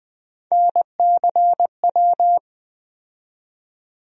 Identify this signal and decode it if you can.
Morse code, heard as a keyed tone, about 20 wpm, 700 Hz. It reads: NCW
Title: USA Call Sign Postfixes - 3 Letters 20wpm